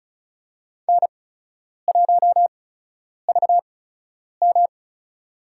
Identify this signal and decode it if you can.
Morse code, heard as a keyed tone, about 36 wpm, spaced as for 10 wpm, 700 Hz. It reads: N1VM
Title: USA Call Signs - 4 Characters - 8x Character Spacing - 35wpm